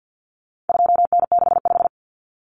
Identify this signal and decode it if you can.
Morse code, heard as a keyed tone, about 49 wpm, 700 Hz. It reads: 2N65
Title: Common Cut Numbers - 4-Digit 50wpm